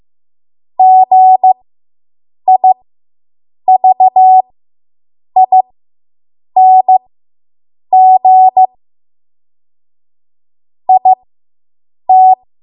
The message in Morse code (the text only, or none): GIVING IT